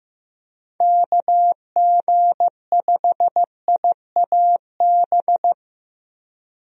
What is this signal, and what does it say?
Morse code, heard as a keyed tone, about 15 wpm, 700 Hz. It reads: KG5IAB
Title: USA Call Signs - 6 Characters - 15wpm